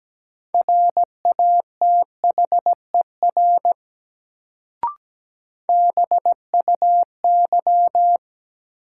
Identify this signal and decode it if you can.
Morse code, heard as a keyed tone, about 17 wpm, 700 Hz. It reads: RATHER BUY
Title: Sentences from Top 700 Words - New Word Drill - 17wpm